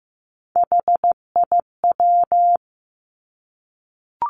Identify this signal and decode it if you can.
Morse code, heard as a keyed tone, about 15 wpm, 700 Hz. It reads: HIW
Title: Three Challenging Letter-Number Combinations 15wpm